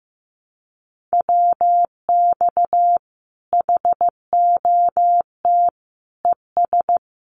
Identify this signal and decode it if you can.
Morse code, heard as a keyed tone, about 15 wpm, 700 Hz. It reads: WX HOT ES
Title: Common QSO Phrases 15wpm